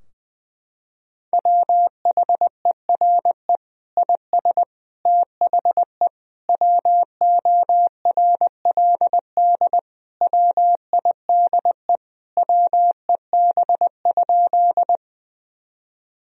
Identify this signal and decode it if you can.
Morse code, heard as a keyed tone, about 20 wpm, 700 Hz. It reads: WHERE IS THE WORLD WIDE WEB?
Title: Sentences from Top 1000 Words - Review Part 4 - 20wpm